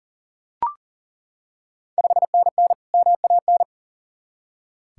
Morse code, heard as a keyed tone, about 40 wpm, 690 Hz. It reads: 5NN MAN
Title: Ontario QSO Party Exchange 40wpm